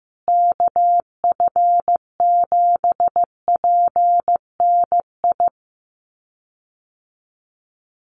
Morse code, heard as a keyed tone, about 15 wpm, 700 Hz. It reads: KF7PNI
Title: USA Call Signs Repeated 3x 15wpm